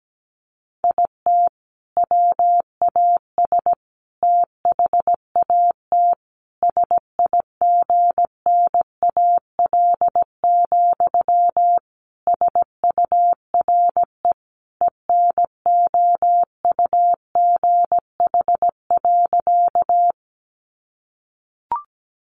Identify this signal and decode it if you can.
Morse code, heard as a keyed tone, about 17 wpm, 700 Hz. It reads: IT WAS THAT SIGNAL, SURE ENOUGH.
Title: Sentences from Top 1400 Words 17wpm